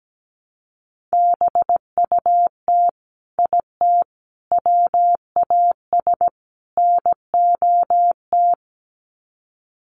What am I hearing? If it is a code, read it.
Morse code, heard as a keyed tone, about 17 wpm, 700 Hz. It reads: BUT IT WAS NOT
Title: Sets of 4 Words 17wpm